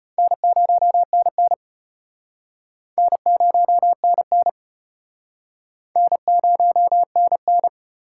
Morse code, heard as a keyed tone, about 36 wpm, 700 Hz. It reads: N0NN N0NN N0NN